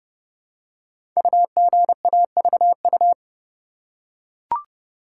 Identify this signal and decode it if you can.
Morse code, heard as a keyed tone, about 30 wpm, 700 Hz. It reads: UGAVU
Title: Code Groups - 5 Letter:Numbers - 30wpm